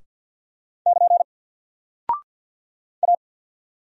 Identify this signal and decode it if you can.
Morse code, heard as a keyed tone, about 48 wpm, 700 Hz. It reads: <KN> A
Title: Single Character (Letter:Number:Punctuation:Prosign) Alt 50wpm